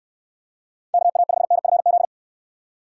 Morse code, heard as a keyed tone, about 69 wpm, 700 Hz. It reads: KA5NFZ